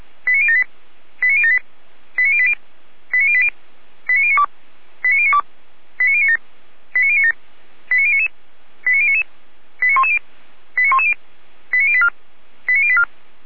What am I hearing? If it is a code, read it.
Five-tone selective call (ZVEI), heard as quick runs of five tones, tones of about 70 ms. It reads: 89078 89078 89080 89080 89092 89092 89097 89097 89099 89099 89110 89110 89084 89084